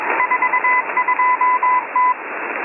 Morse code, heard as a keyed tone, about 21 wpm, 1000 Hz, with radio noise. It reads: 42T
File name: M01- -1810utc-4589usb-msg-byDanix